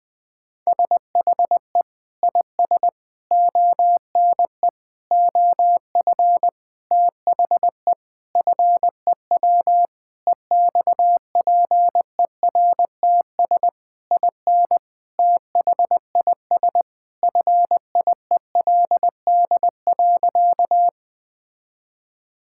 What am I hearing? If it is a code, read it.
Morse code, heard as a keyed tone, about 20 wpm, 700 Hz. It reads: SHE IS ONE OF THE FEW EXPERTS IN THIS FIELD.
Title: Sentences with 5-Word Phrases - Spoken Beginning - 20wpm